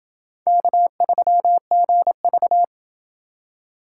Morse code, heard as a keyed tone, about 28 wpm, 700 Hz. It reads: K3GV